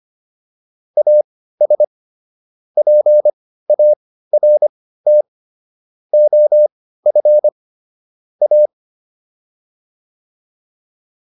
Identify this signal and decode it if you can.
Morse code, heard as a keyed tone, about 25 wpm, spaced as for 15 wpm, 600 Hz. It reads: AS PART OF A